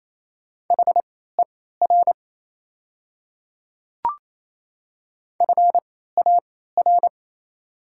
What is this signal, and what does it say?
Morse code, heard as a keyed tone, about 28 wpm, spaced as for 16 wpm, 700 Hz. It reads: HER FAR
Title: 3x Character Spacing - Top 500 Words - 28wpm